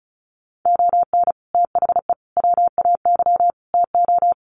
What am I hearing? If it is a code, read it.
Morse code, heard as a keyed tone, about 34 wpm, 700 Hz. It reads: ON THE WAY TO
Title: Sets of 4 Words 35wpm